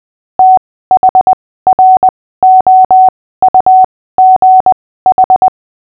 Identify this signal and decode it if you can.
Morse code, heard as a keyed tone, about 20 wpm, spaced as for 15 wpm, 750 Hz. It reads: THROUGH
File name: cw / fists / Short-Story-6-15wpm